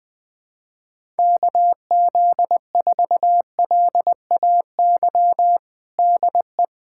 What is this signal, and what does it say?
Morse code, heard as a keyed tone, about 21 wpm, 700 Hz. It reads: KZ4LAY DE
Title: USA Callsign DE Callsign - 6 Characters - 20wpm